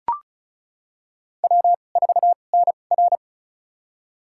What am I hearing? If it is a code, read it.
Morse code, heard as a keyed tone, about 36 wpm, spaced as for 25 wpm, 700 Hz. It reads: W4NR
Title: USA Call Signs - 4 Characters - 2x Character Spacing - 35wpm